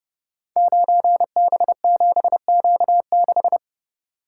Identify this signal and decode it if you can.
Morse code, heard as a keyed tone, about 30 wpm, 700 Hz. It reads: 9B7Q6
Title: Code Groups - 5 Letter:Numbers - 30wpm